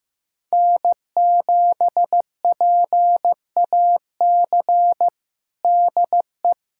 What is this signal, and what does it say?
Morse code, heard as a keyed tone, about 15 wpm, 700 Hz. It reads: N7PAC DE
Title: USA Callsign DE Callsign 15wpm